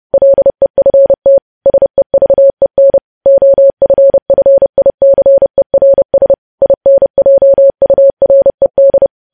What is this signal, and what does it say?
Morse code, heard as a keyed tone, about 30 wpm, 550 Hz. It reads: LEFT SEVEN OFFICERS INJURED
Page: news headlines in morse code at 30 wpm